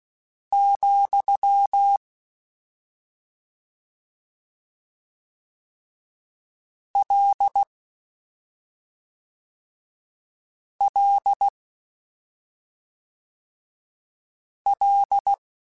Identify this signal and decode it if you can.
Morse code, heard as a keyed tone, about 16 wpm, 800 Hz. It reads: , L L L